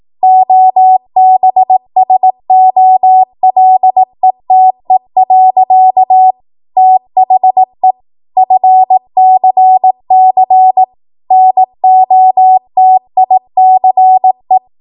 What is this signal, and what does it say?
Morse code, heard as a keyed tone, about 18 wpm, 750 Hz. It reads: OBSOLETE. THE FCC NOTICE
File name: Morse / 18WPM